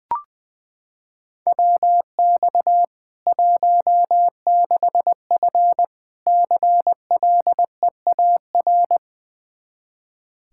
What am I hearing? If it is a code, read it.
Morse code, heard as a keyed tone, about 20 wpm, 700 Hz. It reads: WX 16F CLEAR
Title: Practice QSO - Exchange 2 - Info - 20wpm